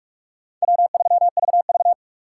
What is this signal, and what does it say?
Morse code, heard as a keyed tone, about 58 wpm, 700 Hz. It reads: W3VV